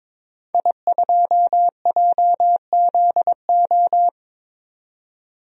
Morse code, heard as a keyed tone, about 22 wpm, 700 Hz. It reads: I2JZO
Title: DXCC-100 Call Signs - 5 Characters - 22wpm